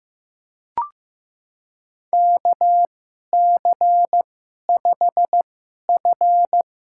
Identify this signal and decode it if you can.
Morse code, heard as a keyed tone, about 16 wpm, spaced as for 11 wpm, 700 Hz. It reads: KC5F
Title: USA Call Signs - 2x Character Spacing - 15wpm